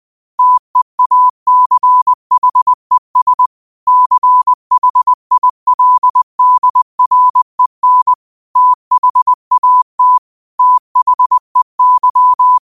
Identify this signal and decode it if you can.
Morse code, heard as a keyed tone, about 20 wpm, 1000 Hz. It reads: TEACHES CHILDREN THAT THEY
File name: qotdfiles / 20wpm